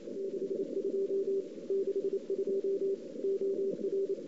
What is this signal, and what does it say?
Morse code, heard as a keyed tone, about 31 wpm, 410 Hz. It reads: 62 62 OR
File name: CW / 2008-ss-cw-k5tr-op-n5rz -02:00